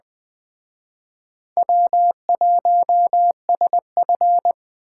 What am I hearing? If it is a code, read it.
Morse code, heard as a keyed tone, about 20 wpm, 700 Hz. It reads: W1SF
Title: Speed Racing - USA 1x2 Call Signs 17wpm